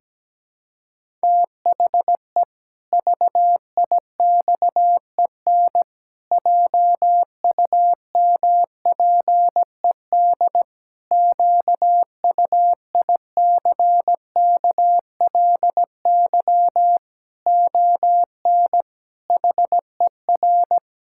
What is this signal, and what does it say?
Morse code, heard as a keyed tone, about 17 wpm, 700 Hz. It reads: THE VIXEN JUMPED QUICKLY ON HER
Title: Pangram Challenge v2 - 17wpm